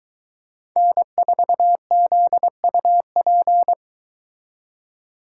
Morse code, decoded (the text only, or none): N4ZUP